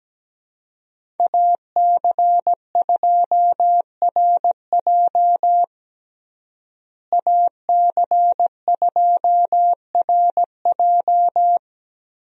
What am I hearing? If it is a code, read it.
Morse code, heard as a keyed tone, about 17 wpm, 700 Hz. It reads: AC2RJ AC2RJ
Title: USA Call Signs - 5 Characters - Repeated 3x - 17wpm